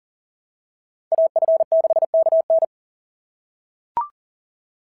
Morse code, heard as a keyed tone, about 40 wpm, 690 Hz. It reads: AF6KN
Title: SOTA QSO Practice 40wpm